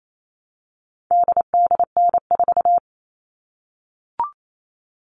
Morse code, heard as a keyed tone, about 28 wpm, 700 Hz. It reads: DDN4
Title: All Cut Numbers - 4-Digit 28wpm